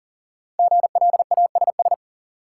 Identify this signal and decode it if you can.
Morse code, heard as a keyed tone, about 37 wpm, 700 Hz. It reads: GLASS
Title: Sentences from Top 1000 Words - Review Part 3 - New Word Drill - 40wpm